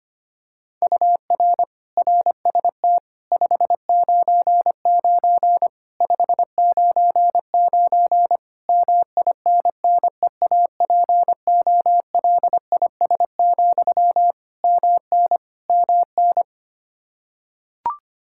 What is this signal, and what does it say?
Morse code, heard as a keyed tone, about 25 wpm, 700 Hz. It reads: UR RST 599 599 MINNEAPOLIS, MN MN
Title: Practice QSO - Exchange 1 - Beginning - 25wpm